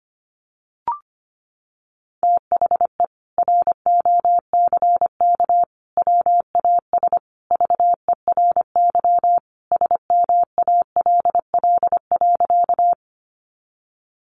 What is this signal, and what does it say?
Morse code, heard as a keyed tone, about 25 wpm, 700 Hz. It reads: THE ROCK WAS VERY SMALL.
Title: Sentences from Top 400 Words 25wpm